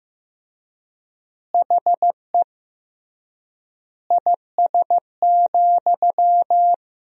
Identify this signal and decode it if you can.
Morse code, heard as a keyed tone, about 15 wpm, 700 Hz. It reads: HE IS,
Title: 3x Word-Spacing - Sentences from Top 100 Words - 15wpm